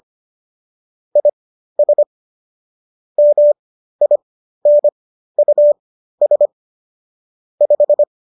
Morse code, heard as a keyed tone, about 25 wpm, spaced as for 13 wpm, 590 Hz. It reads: IS MINUS 5